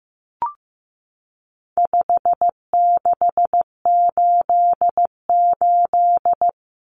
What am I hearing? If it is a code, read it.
Morse code, heard as a keyed tone, about 15 wpm, 700 Hz. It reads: 5688
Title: Common Cut Numbers - 4-Digit 15wpm